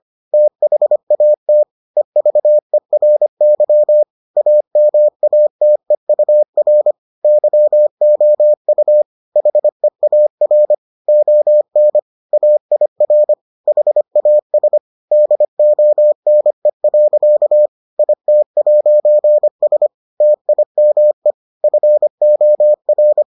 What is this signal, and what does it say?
Morse code, heard as a keyed tone, about 25 wpm, 600 Hz. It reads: THAT EVERY AMATEUR YOU HEAR ON AIR HAS DONE. IT'S TIME FOR